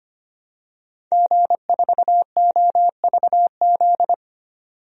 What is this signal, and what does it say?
Morse code, heard as a keyed tone, about 25 wpm, 700 Hz. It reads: G4OVZ
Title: DXCC-100 Call Signs - 5 Characters - 25wpm